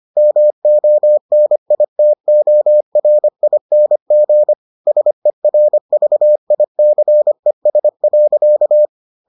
Morse code, heard as a keyed tone, about 25 wpm, 600 Hz. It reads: MONITORING SERVICES.